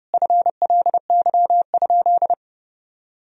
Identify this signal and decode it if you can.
Morse code, heard as a keyed tone, about 30 wpm, 700 Hz. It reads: FLY?
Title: Sentences from Top 1000 Words - Review Part 2 - 30wpm